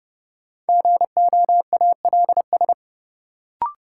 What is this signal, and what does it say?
Morse code, heard as a keyed tone, about 32 wpm, 700 Hz. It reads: GOALS
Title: Sentences from Top 1000 Words - Review Part 6 - New Word Drill - 30wpm